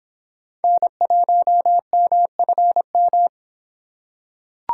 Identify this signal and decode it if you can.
Morse code, heard as a keyed tone, about 24 wpm, 700 Hz. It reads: N1MFM